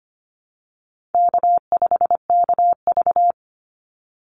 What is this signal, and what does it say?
Morse code, heard as a keyed tone, about 25 wpm, 700 Hz. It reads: K5KV
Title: Easy Call Signs Repeated 2x - 25wpm